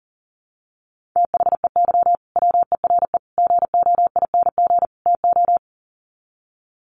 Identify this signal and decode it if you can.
Morse code, heard as a keyed tone, about 40 wpm, 700 Hz. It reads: THEY WERE GOING TO
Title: Sets of 4 Words 40wpm